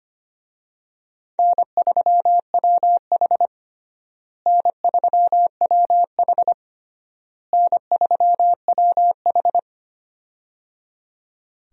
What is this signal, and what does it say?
Morse code, heard as a keyed tone, about 25 wpm, 700 Hz. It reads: N3WH N3WH N3WH
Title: USA Call Signs - 4 Characters - Repeated 3x - 25wpm